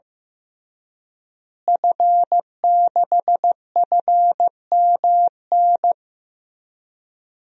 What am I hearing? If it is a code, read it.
Morse code, heard as a keyed tone, about 15 wpm, 700 Hz. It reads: F6FMN
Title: DXCC-100 Call Signs - 5 Characters - Repeated 3x - 15wpm